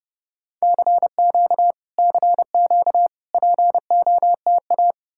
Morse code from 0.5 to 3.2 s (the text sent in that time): CQ CQ